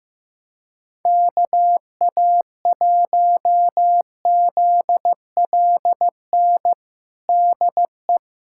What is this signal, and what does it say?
Morse code, heard as a keyed tone, about 16 wpm, 700 Hz. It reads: KA1ZLN DE
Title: USA Callsign DE Callsign - 6 Characters - 15wpm